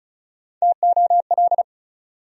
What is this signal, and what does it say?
Morse code, heard as a keyed tone, about 34 wpm, 700 Hz. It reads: TOL
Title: USA Call Sign Postfixes - 3 Letters 35wpm